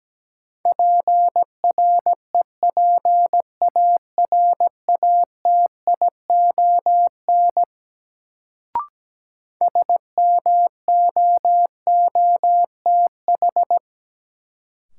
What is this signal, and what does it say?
Morse code, heard as a keyed tone, about 17 wpm, 700 Hz. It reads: PREPARATION SMOOTH
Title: Sentences from Top 1400 Words - New Word Drill - 17wpm